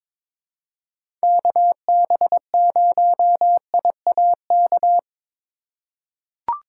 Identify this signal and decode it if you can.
Morse code, heard as a keyed tone, about 22 wpm, 700 Hz. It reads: KB0IAK
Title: USA Call Signs - 6 Characters - 4x Character Spacing - 22wpm